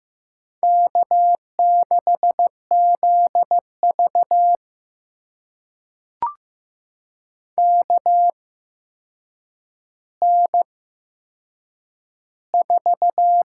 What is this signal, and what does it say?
Morse code, heard as a keyed tone, about 15 wpm, spaced as for 7 wpm, 700 Hz. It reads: K6ZV KN4
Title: USA Call Signs - 8x Character Spacing - 15wpm